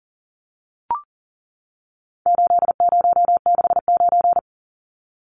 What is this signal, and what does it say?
Morse code, heard as a keyed tone, about 40 wpm, 700 Hz. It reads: 8069
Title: Numbers - 4-Digit 40wpm